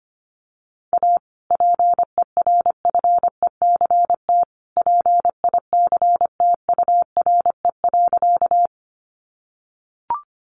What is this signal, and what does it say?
Morse code, heard as a keyed tone, about 25 wpm, 700 Hz. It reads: A PERFECT PICTURE.
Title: Sentences from Top 1000 Words 25wpm